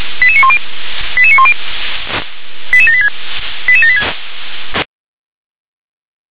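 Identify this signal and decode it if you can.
Five-tone selective call (ZVEI), heard as quick runs of five tones, tones of about 70 ms. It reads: 88910 88910 88786 88786
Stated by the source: ZVEI five-tone call received by SDR.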